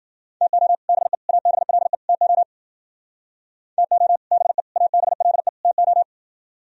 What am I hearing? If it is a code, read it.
Morse code, heard as a keyed tone, about 51 wpm, 700 Hz. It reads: TO BE ABLE TO TO BE ABLE TO